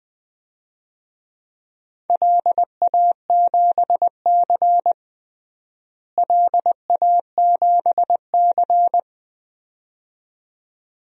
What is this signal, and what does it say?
Morse code, heard as a keyed tone, about 20 wpm, 700 Hz. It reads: LA7C LA7C
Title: DXCC-100 Call Signs - 4 Characters - Repeated 2x - 20wpm